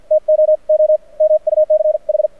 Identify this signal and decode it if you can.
Morse code, heard as a keyed tone, about 48 wpm, spaced as for 41 wpm, 620 Hz. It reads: TOO MUCH
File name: archived / CRSnet